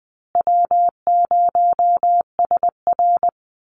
Morse code, heard as a keyed tone, about 20 wpm, 700 Hz. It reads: W0SR
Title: USA Call Signs Repeated 3x 20wpm